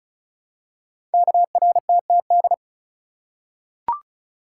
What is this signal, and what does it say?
Morse code, heard as a keyed tone, about 36 wpm, 700 Hz. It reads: KRTTD
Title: Code Groups - 5 Letter:Numbers - 35wpm